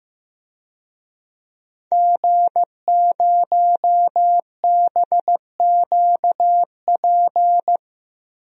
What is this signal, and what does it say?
Morse code, heard as a keyed tone, about 16 wpm, 700 Hz. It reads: G0BQP
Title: DXCC-100 Call Signs - 5 Characters - Repeated 3x - 15wpm